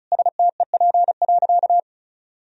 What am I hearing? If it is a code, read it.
Morse code, heard as a keyed tone, about 36 wpm, 700 Hz. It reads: STEP.
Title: Sentences from Top 1000 Words - Review Part 2 - 35wpm